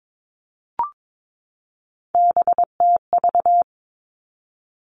Morse code, heard as a keyed tone, about 22 wpm, 700 Hz. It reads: BTV
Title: Three Challenging Letter-Number Combinations 22wpm